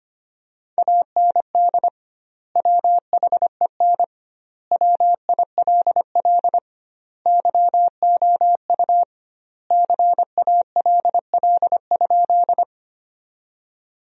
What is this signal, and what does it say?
Morse code, heard as a keyed tone, about 25 wpm, 700 Hz. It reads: AND WHEN WILL YOU CALL?
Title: 2x Word-Spacing - Sentences from Top 100 Words - 25wpm